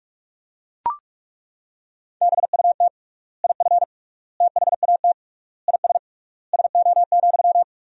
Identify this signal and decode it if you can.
Morse code, heard as a keyed tone, about 41 wpm, 700 Hz. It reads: BUT IF THAT IS SO,
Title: 3x Word-Spacing - Sentences from Top 200 Words - 45wpm